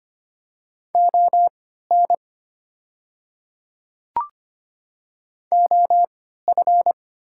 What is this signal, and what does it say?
Morse code, heard as a keyed tone, about 25 wpm, spaced as for 14 wpm, 700 Hz. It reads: ON OF
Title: 3x Character Spacing - Top 200 Words - 25wpm